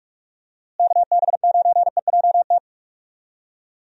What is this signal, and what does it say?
Morse code, heard as a keyed tone, about 46 wpm, 700 Hz. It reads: KB9EJT
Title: USA Call Signs - 6 Characters - Repeated 3x - 45wpm